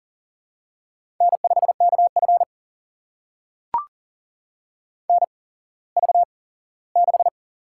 Morse code, heard as a keyed tone, about 41 wpm, spaced as for 19 wpm, 700 Hz. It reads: N5KF NV6
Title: USA Call Signs - 4 Characters - 8x Character Spacing - 40wpm